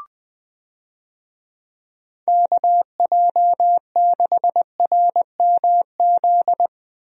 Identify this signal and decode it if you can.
Morse code, heard as a keyed tone, about 21 wpm, 700 Hz. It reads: KJ6RMZ
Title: USA Call Signs - 6 Characters - Repeated 3x - 20wpm